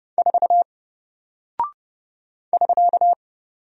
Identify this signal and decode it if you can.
Morse code, heard as a keyed tone, about 29 wpm, 700 Hz. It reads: 4 <SK>
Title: Single Character (Letter:Number:Punctuation:Prosign) Alt 30wpm